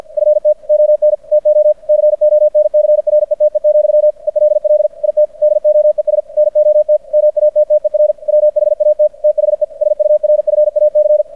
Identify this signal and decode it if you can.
Morse code, heard as a keyed tone, about 50 wpm, 600 Hz. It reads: FT ON TO CONCRETE, ELL IT DOES NOT MATTER WHAT THE SURFAC
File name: archived / CRSnet